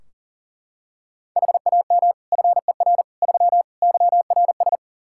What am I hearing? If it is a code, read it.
Morse code, heard as a keyed tone, about 40 wpm, 700 Hz. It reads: HAM FER 3 YRS
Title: Practice QSO - Exchange 3 - More Info - 40wpm